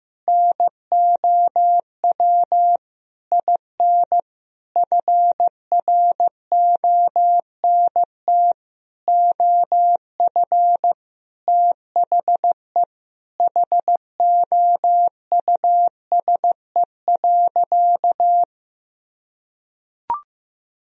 Morse code, heard as a keyed tone, about 15 wpm, 700 Hz. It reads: NOW IN FRONT OF THE HOUSE.
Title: Sentences from Top 1000 Words - Review Part 2 - 15wpm